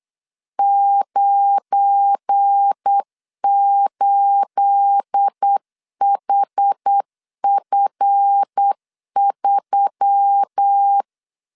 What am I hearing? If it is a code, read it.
Morse code, heard as a keyed tone, about 8 wpm, 800 Hz. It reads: 98HF3
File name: CW / DET / 1 9 1500C S C 8wpm 5653